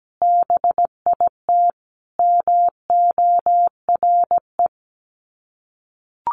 Morse code, heard as a keyed tone, about 17 wpm, 700 Hz. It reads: BIT MORE
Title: Sets of 4 Words 17wpm